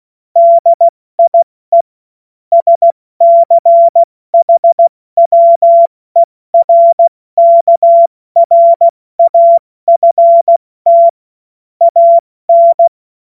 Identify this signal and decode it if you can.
Morse code, heard as a keyed tone, about 16 wpm, 680 Hz. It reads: DIE SCHWERKRAFT AN